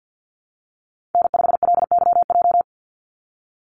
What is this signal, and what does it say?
Morse code, heard as a keyed tone, about 50 wpm, 700 Hz. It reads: N5LYJ
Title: Call Signs Repeated 2x - 50wpm